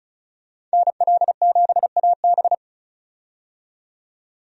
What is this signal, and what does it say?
Morse code, heard as a keyed tone, about 36 wpm, 700 Hz. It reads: NL7AB
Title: USA Call Signs - 5 Characters - 35wpm